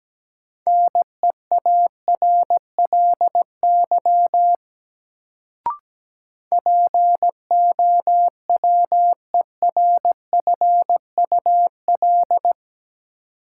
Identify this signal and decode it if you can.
Morse code, heard as a keyed tone, about 17 wpm, 700 Hz. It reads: NEARLY POWERFUL
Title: Sentences from Top 1300 Words - New Word Drill Alt - 17wpm